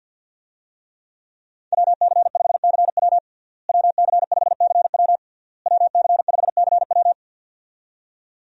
Morse code, heard as a keyed tone, about 52 wpm, 700 Hz. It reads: WC5CW WC5CW WC5CW
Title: USA Call Signs - 5 Characters - Repeated 3x - 50wpm